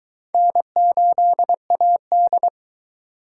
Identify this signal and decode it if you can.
Morse code, heard as a keyed tone, about 24 wpm, 700 Hz. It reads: N8AD